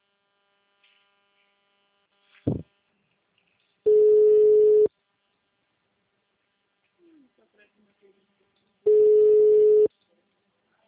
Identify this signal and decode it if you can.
Ringback tone (Europe, 425 Hz), with speech.